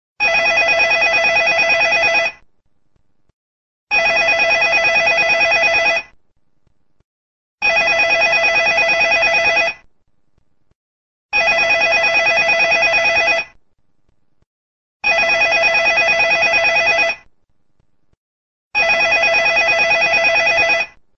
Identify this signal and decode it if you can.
There is no code.